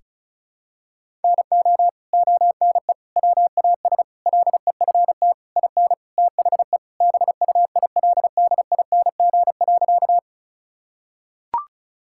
Morse code, heard as a keyed tone, about 34 wpm, 700 Hz. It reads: NO ONE WAS LEFT IN THE BUILDING.
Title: Sentences from Top 1000 Words - Review Part 5 - 35wpm